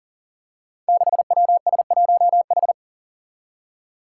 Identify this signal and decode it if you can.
Morse code, heard as a keyed tone, about 40 wpm, 700 Hz. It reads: 6WS1H
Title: Five Challenging Letter-Number Combinations 40wpm